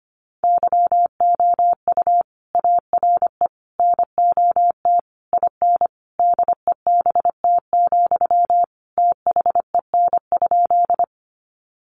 Morse code, heard as a keyed tone, about 25 wpm, 700 Hz. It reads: YOU ARE NOT IN DEBT, THEN?
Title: Sentences from Top 1300 Words 25wpm